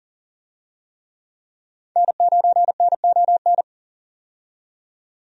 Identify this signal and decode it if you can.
Morse code, heard as a keyed tone, about 40 wpm, 700 Hz. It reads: N9NON
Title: USA Call Signs - 5 Characters - 40wpm